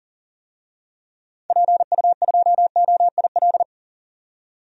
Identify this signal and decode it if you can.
Morse code, heard as a keyed tone, about 40 wpm, 700 Hz. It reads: PU2OIL
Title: DXCC-100 Call Signs - 6 Characters - 40wpm